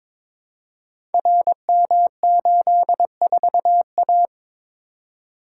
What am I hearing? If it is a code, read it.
Morse code, heard as a keyed tone, about 22 wpm, 700 Hz. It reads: RM84A
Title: Code Groups - 5 Letter:Numbers - 22wpm